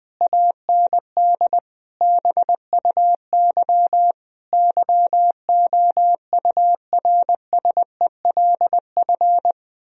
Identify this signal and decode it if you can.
Morse code, heard as a keyed tone, about 20 wpm, 700 Hz. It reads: AND BUY YOURSELF
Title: Sentences from Top 1000 Words - Review Part 4 - 20wpm